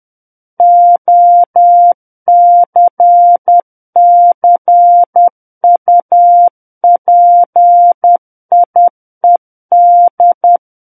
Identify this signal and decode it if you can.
Morse code, heard as a keyed tone, about 10 wpm, 700 Hz. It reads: OCCUPIED
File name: Book One / 01-Chapter 1 - 10 WPM